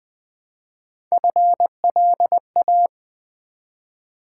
Morse code, heard as a keyed tone, about 20 wpm, 700 Hz. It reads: FLA
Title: Build A Word - 1 Letter At A Time - 4-Letter Words - 20wpm